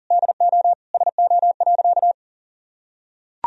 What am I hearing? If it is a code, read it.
Morse code, heard as a keyed tone, about 37 wpm, 700 Hz. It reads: DO SO.
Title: Sentences from Top 1000 Words - Review Part 2 - 40wpm